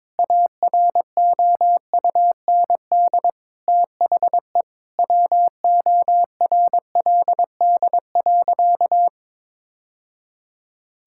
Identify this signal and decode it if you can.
Morse code, heard as a keyed tone, about 22 wpm, 700 Hz. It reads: AROUND THE WORLD.
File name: practice / 22wpm / Sentences with 3-Word Phrases - 22wpm